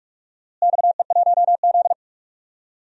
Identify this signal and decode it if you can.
Morse code, heard as a keyed tone, about 45 wpm, 700 Hz. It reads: XE1Z
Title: DXCC-100 Call Signs - 4 Characters - Repeated 3x - 45wpm